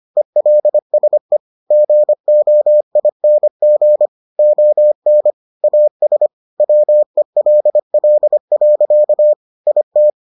Morse code, heard as a keyed tone, about 25 wpm, 600 Hz. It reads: ELSE GOING ON AS WELL. IT